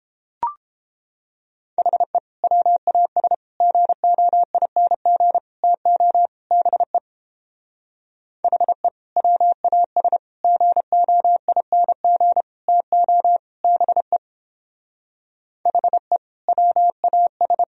Morse code, heard as a keyed tone, about 31 wpm, 700 Hz. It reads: HE WAS GOING TO BE HE WAS GOING TO BE HE WAS